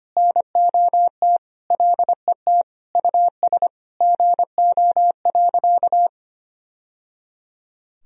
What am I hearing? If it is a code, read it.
Morse code, heard as a keyed tone, about 25 wpm, 700 Hz. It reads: NOT LET US GO.
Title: Sentences from Top 1000 Words - Review Part 2 - 25wpm